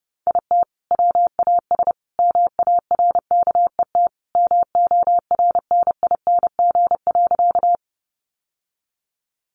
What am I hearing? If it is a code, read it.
Morse code, heard as a keyed tone, about 30 wpm, 700 Hz. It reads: IT WAS MARKET MORNING.
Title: Sentences from Top 600 Words 30wpm